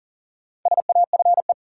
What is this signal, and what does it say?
Morse code, heard as a keyed tone, about 40 wpm, 700 Hz. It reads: SAFE
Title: Four Letter Words - 40wpm